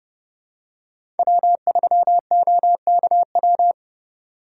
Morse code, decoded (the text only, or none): W3OKW